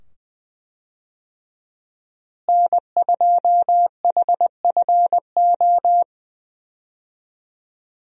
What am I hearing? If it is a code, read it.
Morse code, heard as a keyed tone, about 20 wpm, 700 Hz. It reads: N2HFO
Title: USA Call Signs - 5 Characters - Repeated 2x - 20wpm